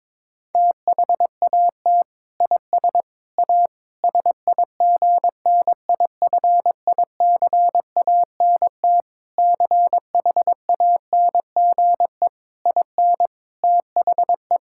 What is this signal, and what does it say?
Morse code, heard as a keyed tone, about 22 wpm, 700 Hz. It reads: THAT IS A SIGNIFICANT CHANGE IN THE
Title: Sentences from Top 1000 Words - Review Part 6 - 22wpm